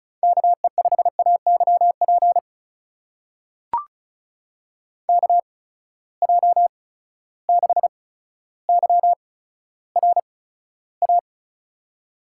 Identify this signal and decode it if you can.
Morse code, heard as a keyed tone, about 36 wpm, spaced as for 15 wpm, 700 Hz. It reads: KE5AYP KJ6YRA
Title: USA Call Signs - 6 Characters - 8x Character Spacing - 35wpm